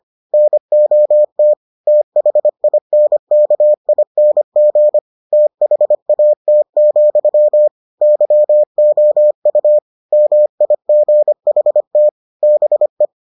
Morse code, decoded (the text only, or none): NOT THINKING THAT, YOU MIGHT BE